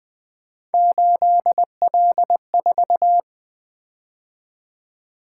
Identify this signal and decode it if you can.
Morse code, heard as a keyed tone, about 20 wpm, 700 Hz. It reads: 8L4
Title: Three Letter-Number Combinations - 20wpm